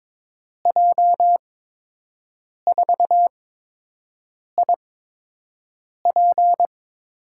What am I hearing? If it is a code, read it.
Morse code, heard as a keyed tone, about 22 wpm, spaced as for 6 wpm, 700 Hz. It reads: J4IP
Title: USA Call Signs - 6 Characters - 8x Character Spacing - 22wpm